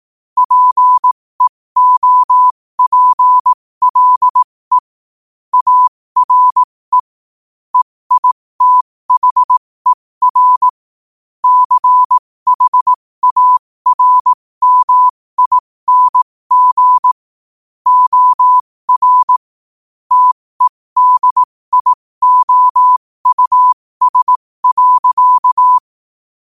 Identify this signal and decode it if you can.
Morse code, heard as a keyed tone, about 18 wpm, spaced as for 15 wpm, 1000 Hz. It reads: PEOPLE ARE EITHER CHARMING OR TEDIOUS.
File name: qotdfiles / 15wpm